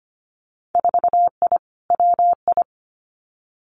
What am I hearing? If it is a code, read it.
Morse code, heard as a keyed tone, about 25 wpm, 700 Hz. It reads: 4I WI
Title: Winter Field Day Exchanges 25wpm